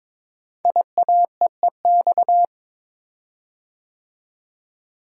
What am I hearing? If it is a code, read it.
Morse code, heard as a keyed tone, about 22 wpm, 700 Hz. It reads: IAEEX
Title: Code Groups - 5 Letter:Numbers - 22wpm